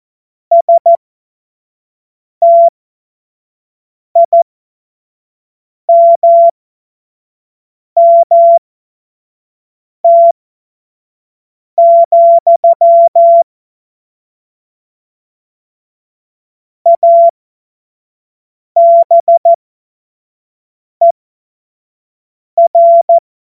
Morse code, heard as a keyed tone, about 14 wpm, spaced as for 5 wpm, 680 Hz. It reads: STIMMT, ABER
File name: Morsekurs / Witze / Witz 08. Februar 2022 25 F70